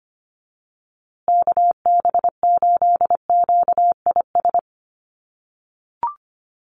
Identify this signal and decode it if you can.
Morse code, heard as a keyed tone, about 25 wpm, 700 Hz. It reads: KB8QIS
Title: USA Call Signs Repeated 3x 25wpm